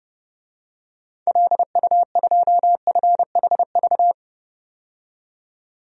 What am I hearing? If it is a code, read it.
Morse code, heard as a keyed tone, about 30 wpm, 700 Hz. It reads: LU2FHV
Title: DXCC-100 Call Signs - 6 Characters - 30wpm